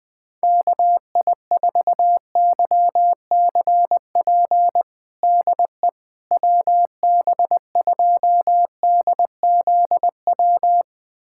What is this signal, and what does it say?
Morse code, heard as a keyed tone, about 21 wpm, 700 Hz. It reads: KI4YCP DE WB2DZW
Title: USA Callsign DE Callsign - 6 Characters - 20wpm